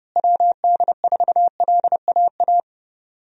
Morse code, heard as a keyed tone, about 30 wpm, 700 Hz. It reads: WD4LAA
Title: USA Call Signs - 6 Characters - Repeated 3x - 30wpm